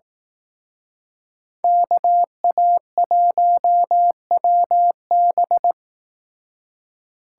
Morse code, heard as a keyed tone, about 18 wpm, 700 Hz. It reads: KA1WB
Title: Speed Racing - USA 2x2 Call Signs 15wpm